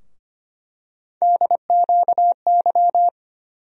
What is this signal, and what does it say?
Morse code, heard as a keyed tone, about 25 wpm, 700 Hz. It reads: DQY
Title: USA Call Sign Postfixes - 3 Letters 25wpm